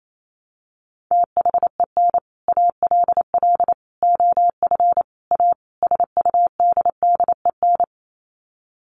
Morse code, heard as a keyed tone, about 28 wpm, 700 Hz. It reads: THEN ALL OF A SUDDEN
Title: Sets of 5 Words Encore 28wpm